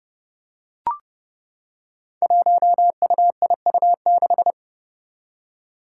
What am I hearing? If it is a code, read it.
Morse code, heard as a keyed tone, about 28 wpm, 700 Hz. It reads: 1UIU6
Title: Five Challenging Letter-Number Combinations 30wpm